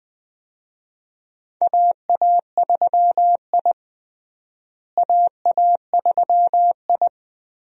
Morse code, heard as a keyed tone, about 20 wpm, 700 Hz. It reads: AA3I AA3I
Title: USA Call Signs - 4 Characters - Repeated 2x - 20wpm